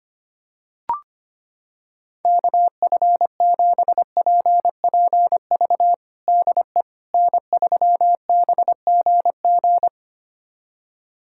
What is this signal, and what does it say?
Morse code, heard as a keyed tone, about 25 wpm, 700 Hz. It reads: KF7PPV DE N3BGG
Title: USA Callsign DE Callsign 25wpm